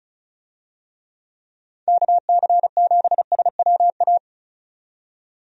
Morse code, heard as a keyed tone, about 36 wpm, 700 Hz. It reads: KC7SWA
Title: USA Call Signs - 6 Characters - 35wpm